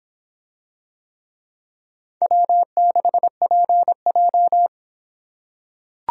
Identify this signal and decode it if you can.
Morse code, heard as a keyed tone, about 26 wpm, 700 Hz. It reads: W6PJ